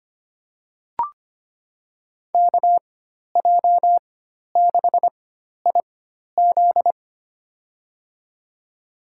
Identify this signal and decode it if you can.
Morse code, heard as a keyed tone, about 25 wpm, spaced as for 12 wpm, 700 Hz. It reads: KJ6IZ
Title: USA Call Signs - 5 Characters - 4x Character Spacing - 25wpm